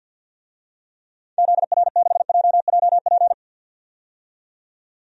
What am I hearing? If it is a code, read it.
Morse code, heard as a keyed tone, about 50 wpm, 700 Hz. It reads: ZR6JJP
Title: DXCC-100 Call Signs - 6 Characters - 50wpm